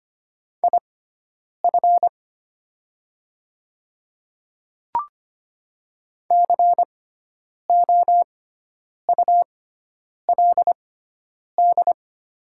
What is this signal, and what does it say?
Morse code, heard as a keyed tone, about 25 wpm, spaced as for 9 wpm, 700 Hz. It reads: IF COULD
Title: 6x Character Spacing - Top 100 Words - 25wpm